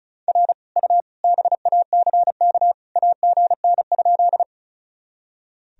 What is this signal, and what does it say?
Morse code, heard as a keyed tone, about 36 wpm, 700 Hz. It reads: R U BACK AGN?
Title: Sentences from Top 200 Words and Abbreviations 35wpm